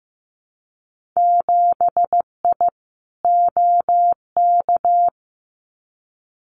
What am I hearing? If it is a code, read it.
Morse code, heard as a keyed tone, about 15 wpm, 700 Hz. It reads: 7I OK
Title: Winter Field Day Exchanges Encore 15wpm